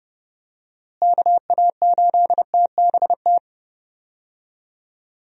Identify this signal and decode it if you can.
Morse code, heard as a keyed tone, about 30 wpm, 700 Hz. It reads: KA8TBT
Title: USA Call Signs - 6 Characters - Repeated 2x - 30wpm